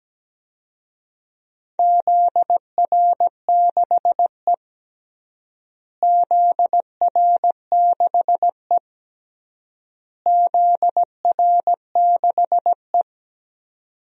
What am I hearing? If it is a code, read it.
Morse code, heard as a keyed tone, about 17 wpm, 700 Hz. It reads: ZR6E ZR6E ZR6E
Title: DXCC-100 Call Signs - 4 Characters - Repeated 3x - 17wpm